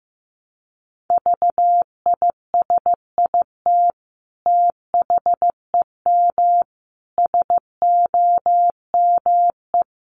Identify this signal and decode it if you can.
Morse code, heard as a keyed tone, about 15 wpm, 700 Hz. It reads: VISIT THEM SOME
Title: Sentences from Top 900 Words 15wpm